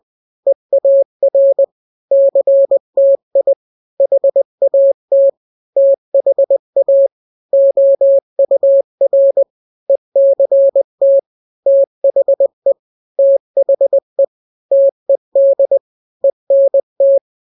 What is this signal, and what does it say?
Morse code, heard as a keyed tone, about 20 wpm, 550 Hz. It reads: EAR CTI HAT THA OUR ECT THE THE TED ENT
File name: n-grams / 3-grams 20 18 WPM